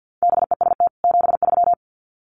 Morse code, heard as a keyed tone, about 50 wpm, 700 Hz. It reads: BEST 73
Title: Common QSO Phrases 50wpm